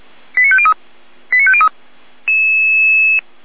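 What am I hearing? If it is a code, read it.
Five-tone selective call (ZVEI), heard as quick runs of five tones, tones of about 70 ms. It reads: 89503 89503
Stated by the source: Five-tone ZVEI selective call received by SDR.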